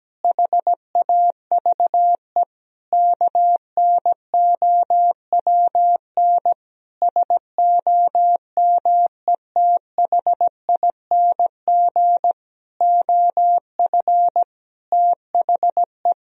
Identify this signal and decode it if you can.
Morse code, heard as a keyed tone, about 17 wpm, 700 Hz. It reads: HAVE KNOWN SOMETHING OF THE
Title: Sentences from Top 1000 Words - Review Part 3 - 17wpm